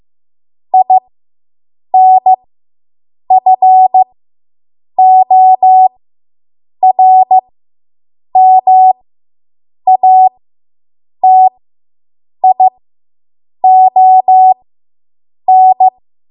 Morse code, heard as a keyed tone, about 15 wpm, spaced as for 7 wpm, 750 Hz. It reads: INFORMATION